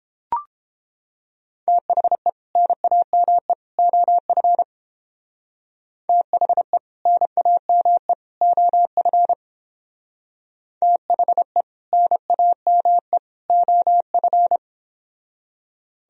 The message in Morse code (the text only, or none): THE NAME OF THE NAME OF THE NAME OF